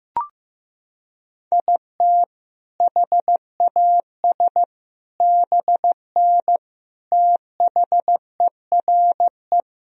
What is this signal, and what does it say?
Morse code, heard as a keyed tone, about 16 wpm, 700 Hz. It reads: IT HAS BN THERE
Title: Sentences from Top 400 Words and Abbreviations 15wpm